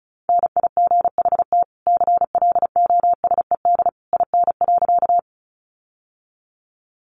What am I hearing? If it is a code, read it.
Morse code, heard as a keyed tone, about 34 wpm, 700 Hz. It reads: NIGHT CLOSED IN.
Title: Sentences from Top 1400 Words 35wpm